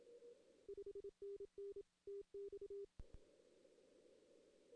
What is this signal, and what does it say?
Morse code, heard as a keyed tone, about 27 wpm, 400 Hz. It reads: 5NN TX
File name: 2016 arrl dx cw / 2016-ARRL-DX-CW-K5TR- - 0-3